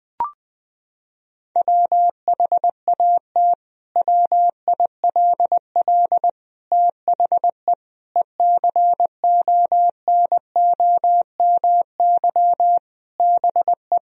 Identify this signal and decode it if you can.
Morse code, heard as a keyed tone, about 20 wpm, 700 Hz. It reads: WHAT WILL THE ECONOMY BE
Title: Sentences from Top 1000 Words - Review Part 3 - 20wpm